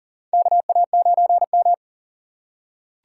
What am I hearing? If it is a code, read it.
Morse code, heard as a keyed tone, about 40 wpm, 700 Hz. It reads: KA9M
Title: USA Call Signs - 4 Characters - 3x Character Spacing - 40wpm